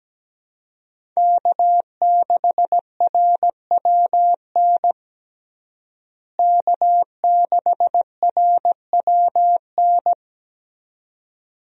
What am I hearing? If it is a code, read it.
Morse code, heard as a keyed tone, about 17 wpm, 700 Hz. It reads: K6RWN K6RWN
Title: USA Call Signs - 5 Characters - Repeated 2x - 17wpm